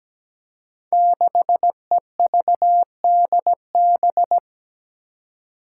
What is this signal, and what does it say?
Morse code, heard as a keyed tone, about 17 wpm, 700 Hz. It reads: 6EVDB
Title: Five Challenging Letter-Number Combinations 17wpm